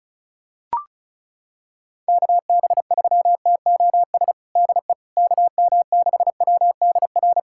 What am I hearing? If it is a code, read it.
Morse code, heard as a keyed tone, about 34 wpm, 700 Hz. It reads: KB3TOS DE KM6WDR
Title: USA Callsign DE Callsign - 6 Characters - 35wpm